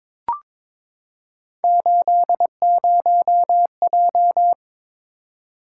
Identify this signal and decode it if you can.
Morse code, heard as a keyed tone, about 22 wpm, 700 Hz. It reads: 80J
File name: practice / 22wpm / Three Letter-Number Combinations - 22wpm